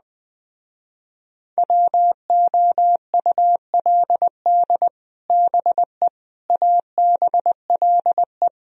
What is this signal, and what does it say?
Morse code, heard as a keyed tone, about 20 wpm, 700 Hz. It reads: WOULD BE ABLE
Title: Speed-Racing - Sets of 4 Words 17wpm